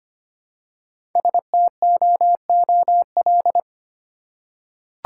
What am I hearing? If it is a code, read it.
Morse code, heard as a keyed tone, about 25 wpm, 700 Hz. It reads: STOOL
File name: practice / 25wpm / Easy Words with Hints - 25wpm